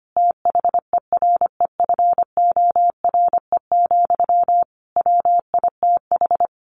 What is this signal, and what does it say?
Morse code, heard as a keyed tone, about 25 wpm, 700 Hz. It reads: THEREFORE, WITH
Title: Sentences from Top 1500 Words 25wpm